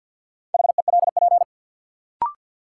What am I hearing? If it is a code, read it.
Morse code, heard as a keyed tone, about 53 wpm, 700 Hz. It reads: HELP
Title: Sentences from Top 200 Words - New Word Drill - 50wpm